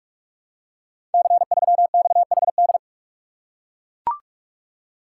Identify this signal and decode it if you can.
Morse code, heard as a keyed tone, about 44 wpm, 690 Hz. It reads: C3XHD
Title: Code Groups - 5 Letter:Numbers - 45wpm